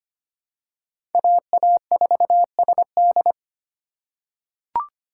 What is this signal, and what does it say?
Morse code, heard as a keyed tone, about 25 wpm, 700 Hz. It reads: AA4SD
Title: USA Call Signs - 5 Characters - 4x Character Spacing - 25wpm